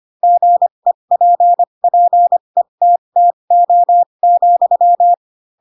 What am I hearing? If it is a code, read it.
Morse code, heard as a keyed tone, about 25 wpm, spaced as for 22 wpm, 700 Hz. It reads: GEPPETTO,
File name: Pinocchio / Pinocchio ch3 18 25 ik1hge